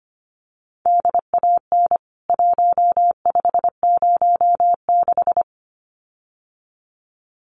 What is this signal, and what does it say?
Morse code, heard as a keyed tone, about 25 wpm, 700 Hz. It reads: DAN 1506
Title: CWT Exchange 25wpm